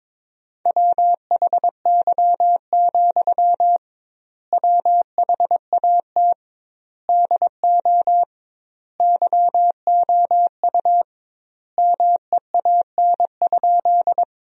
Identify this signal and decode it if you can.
Morse code, heard as a keyed tone, about 22 wpm, 700 Hz. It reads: WHY, WHAT DO YOU MEAN?
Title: 2x Word-Spacing - Sentences from Top 200 Words - 22wpm